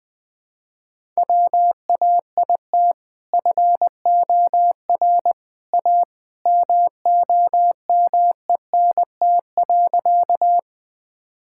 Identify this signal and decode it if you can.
Morse code, heard as a keyed tone, about 21 wpm, 700 Hz. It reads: WAIT FOR A MOMENT.
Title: Sentences with 2-Word Phrases - 20wpm